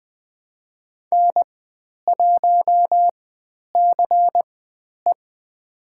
Morse code, heard as a keyed tone, about 20 wpm, spaced as for 10 wpm, 700 Hz. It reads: N1CE
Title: 095 Lesson 16g - Letter c - Call Signs 20wpm